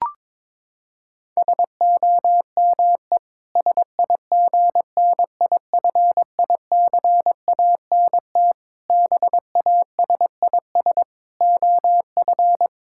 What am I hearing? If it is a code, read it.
Morse code, heard as a keyed tone, about 22 wpm, 700 Hz. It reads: SOME SIGNIFICANT BASIS OF
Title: Sentences from Top 1600 Words 22wpm